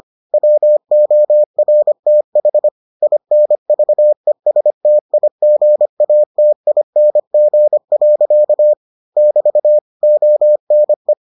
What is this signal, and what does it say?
Morse code, heard as a keyed tone, about 25 wpm, 600 Hz. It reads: WORTH INVESTIGATING. <BT> ONE